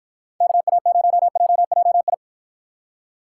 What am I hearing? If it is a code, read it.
Morse code, heard as a keyed tone, about 50 wpm, 700 Hz. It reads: KA0JJI